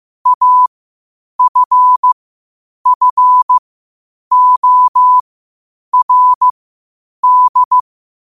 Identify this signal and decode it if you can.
Morse code, heard as a keyed tone, about 15 wpm, spaced as for 8 wpm, 1000 Hz. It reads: AFFORD
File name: qotdfiles / 8wpm